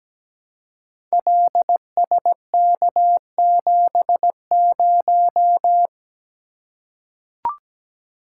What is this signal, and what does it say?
Morse code, heard as a keyed tone, about 17 wpm, 700 Hz. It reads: LSK70
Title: Code Groups - 5 Letter:Numbers - 17wpm